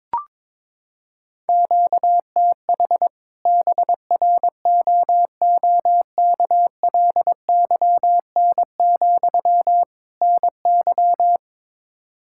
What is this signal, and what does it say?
Morse code, heard as a keyed tone, about 22 wpm, 700 Hz. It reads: QTH BROOKLYN, NY
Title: Practice QSO - Exchange 1 - Beginning - 22wpm